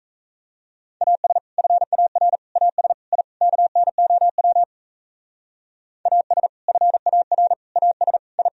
Morse code, heard as a keyed tone, about 36 wpm, 700 Hz. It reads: AS FAR AS I KNOW AS FAR AS I